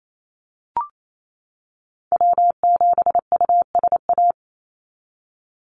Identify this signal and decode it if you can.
Morse code, heard as a keyed tone, about 28 wpm, 700 Hz. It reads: W7USA
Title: NA SOTA Activator Callsigns 28wpm